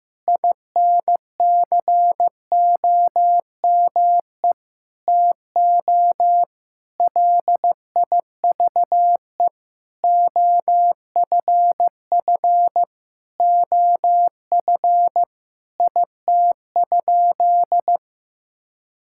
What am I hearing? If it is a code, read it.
Morse code, heard as a keyed tone, about 15 wpm, 700 Hz. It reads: INCOME TO LIVE OFF OF IT?
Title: Sentences from Top 1000 Words - Review Part 5 - 15wpm